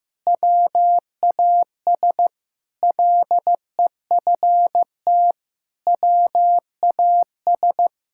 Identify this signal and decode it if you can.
Morse code, heard as a keyed tone, about 15 wpm, 700 Hz. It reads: WAS LEFT WAS
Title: Sentences from Top 1000 Words - Review Part 5 - 15wpm